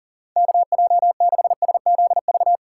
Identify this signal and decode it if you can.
Morse code, heard as a keyed tone, about 49 wpm, 700 Hz. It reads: KJ6SZV
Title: Speed Racing - USA 2x3 Call Signs 40wpm